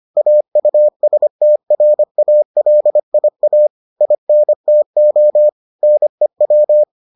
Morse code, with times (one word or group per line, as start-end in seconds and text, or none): 0.2-3.7 AUSTRALIA
4.0-5.5 INTO
5.8-6.8 NEW